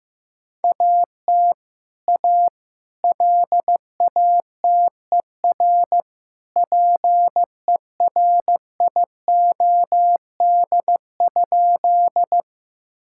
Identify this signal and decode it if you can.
Morse code, heard as a keyed tone, about 15 wpm, 700 Hz. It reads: AT A LATER PERIOD?
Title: Sentences from Top 1000 Words - Review Part 5 - 15wpm